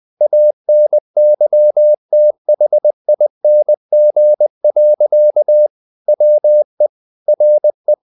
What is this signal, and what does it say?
Morse code, heard as a keyed tone, about 20 wpm, 600 Hz. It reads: ANYTHING. WE RE
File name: the old man and the sea / the old man and the sea part 1 20wpm